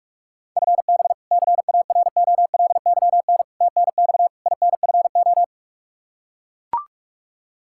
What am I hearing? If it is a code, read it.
Morse code, heard as a keyed tone, about 44 wpm, 700 Hz. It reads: FB CAROLYN TNX INFO
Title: Practice QSO - Exchange 3 - More Info - 45wpm